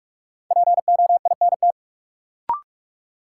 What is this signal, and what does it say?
Morse code, heard as a keyed tone, about 46 wpm, 700 Hz. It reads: POINT
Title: Sentences from Top 200 Words - New Word Drill - 45wpm